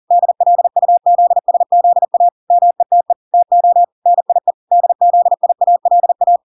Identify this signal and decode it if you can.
Morse code, heard as a keyed tone, about 40 wpm, 700 Hz. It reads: DLUZSZA METE TO NIE DZIALA